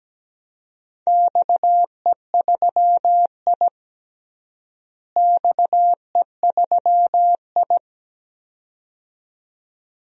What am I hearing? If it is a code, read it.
Morse code, heard as a keyed tone, about 17 wpm, 700 Hz. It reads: XE3I XE3I
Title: DXCC-100 Call Signs - 4 Characters - Repeated 2x - 17wpm